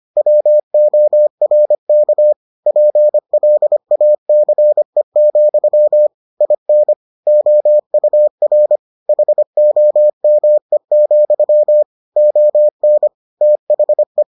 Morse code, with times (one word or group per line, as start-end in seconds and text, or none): 0.2-2.3 WORK
2.7-6.1 PLACE,
6.4-6.9 IN
7.3-8.8 OUR
9.1-11.8 HOME,
12.2-13.1 ON
13.4-14.2 THE